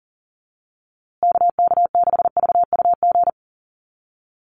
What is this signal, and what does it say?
Morse code, heard as a keyed tone, about 40 wpm, 700 Hz. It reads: KK6VUG
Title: USA Call Signs Repeated 3x 40wpm